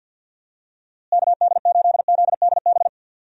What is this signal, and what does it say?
Morse code, heard as a keyed tone, about 52 wpm, 700 Hz. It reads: KD8ZDB
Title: USA Call Signs - 6 Characters - Repeated 3x - 50wpm